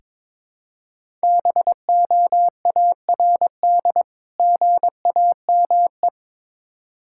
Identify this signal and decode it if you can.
Morse code, heard as a keyed tone, about 22 wpm, 700 Hz. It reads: BOARD GAME
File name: Difficult Words with Hints - 22wpm